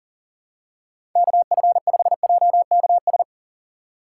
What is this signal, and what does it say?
Morse code, heard as a keyed tone, about 40 wpm, 700 Hz. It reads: KF5JKS
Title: USA Call Signs - 4x Character Spacing - 40wpm